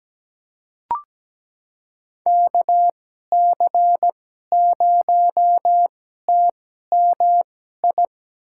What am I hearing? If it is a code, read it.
Morse code, heard as a keyed tone, about 17 wpm, spaced as for 12 wpm, 700 Hz. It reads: KC0TMI
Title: USA Call Signs - 6 Characters - 2x Character Spacing - 17wpm